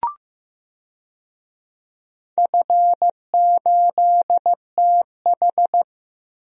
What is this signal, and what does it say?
Morse code, heard as a keyed tone, about 15 wpm, 700 Hz. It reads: F8TH
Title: DXCC-100 Call Signs - 4 Characters - Repeated 2x - 15wpm